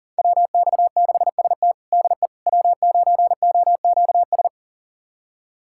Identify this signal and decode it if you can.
Morse code, heard as a keyed tone, about 40 wpm, 700 Hz. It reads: WX6ST DE W9OQS
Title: USA Callsign DE Callsign - 5 Characters - 40wpm